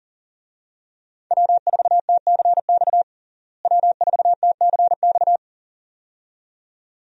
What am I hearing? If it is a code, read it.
Morse code, heard as a keyed tone, about 40 wpm, 700 Hz. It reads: W4TCX W4TCX
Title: USA Call Signs - 5 Characters - Repeated 2x - 40wpm